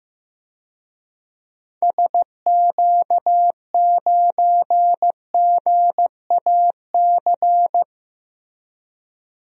Morse code, heard as a keyed tone, about 15 wpm, 700 Hz. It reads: SQ9GAC
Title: DXCC-100 Call Signs - 6 Characters - Repeated 3x - 15wpm